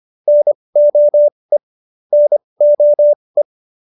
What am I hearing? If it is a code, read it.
Morse code, heard as a keyed tone, about 25 wpm, spaced as for 20 wpm, 590 Hz. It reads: NOE NOE